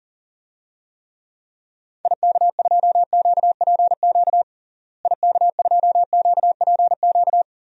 Morse code, heard as a keyed tone, about 40 wpm, 700 Hz. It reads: IK2QPQ IK2QPQ
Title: DXCC-100 Call Signs - 6 Characters - Repeated 2x - 40wpm